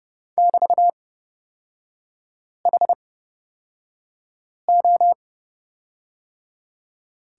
Morse code, heard as a keyed tone, about 30 wpm, spaced as for 5 wpm, 700 Hz. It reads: <BT>HO
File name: Lesson 36e - BT - Three Characters 30wpm